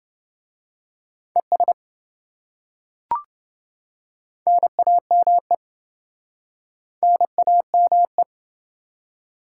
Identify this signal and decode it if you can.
Morse code, heard as a keyed tone, about 27 wpm, 700 Hz. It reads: ES NAME NAME